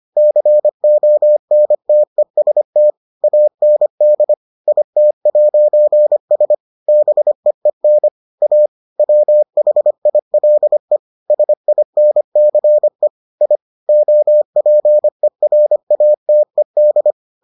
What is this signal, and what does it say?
Morse code, heard as a keyed tone, about 25 wpm, 600 Hz. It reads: CONTEST AND IT'S BEEN A WHILE SINCE I OPERATED